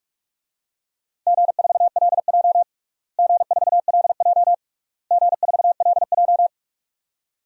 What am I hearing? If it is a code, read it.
Morse code, heard as a keyed tone, about 46 wpm, 700 Hz. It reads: G4LJ G4LJ G4LJ
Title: DXCC-100 Call Signs - 4 Characters - Repeated 3x - 45wpm